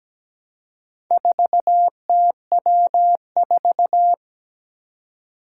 Morse code, heard as a keyed tone, about 17 wpm, 700 Hz. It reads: 4TW4
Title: Four Challenging Letter-Number Combinations 17wpm